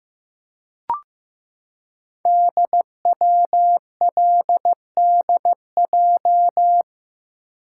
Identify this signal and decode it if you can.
Morse code, heard as a keyed tone, about 15 wpm, 700 Hz. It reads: DWLDJ
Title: Code Groups - 5 Letters - 15wpm